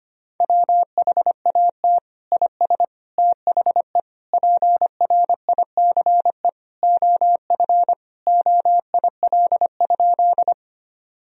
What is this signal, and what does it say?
Morse code, heard as a keyed tone, about 25 wpm, 700 Hz. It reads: WHAT IS THE PRICE OF OIL?
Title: Sentences from Top 1000 Words - Review Part 4 - 25wpm